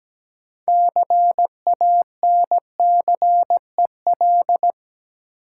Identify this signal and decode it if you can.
Morse code, heard as a keyed tone, about 17 wpm, 700 Hz. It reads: CANCEL
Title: Sentences from Top 1200 Words - New Word Drill Alt - 17wpm